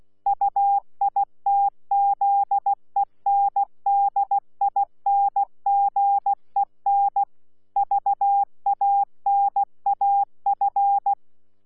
Morse code, heard as a keyed tone, about 16 wpm, 800 Hz. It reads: UITZENDINGEN VANAF
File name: Z PI4VRZ / PI4VRZ 14-jun-2025